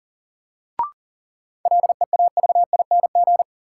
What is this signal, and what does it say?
Morse code, heard as a keyed tone, about 40 wpm, 700 Hz. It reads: LEAVING
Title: Sentences from Top 1000 Words - Review Part 4 - New Word Drill - 40wpm